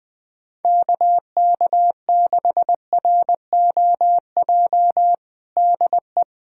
Morse code, heard as a keyed tone, about 20 wpm, 700 Hz. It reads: KK6ROJ DE
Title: USA Callsign DE Callsign - 6 Characters - 20wpm